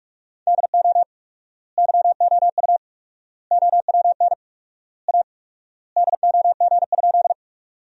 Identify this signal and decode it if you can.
Morse code, heard as a keyed tone, about 46 wpm, 700 Hz. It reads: DO YOU OWN A DOG?
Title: 4x Word-Spacing - Sentences from Top 400 Words - 45wpm